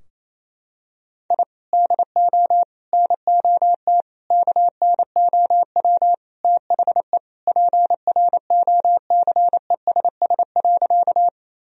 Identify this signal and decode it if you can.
Morse code, heard as a keyed tone, about 28 wpm, 700 Hz. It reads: I DO NOT KNOW THE PROCESS.
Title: Sentences from Top 1000 Words - Review Part 3 - 28wpm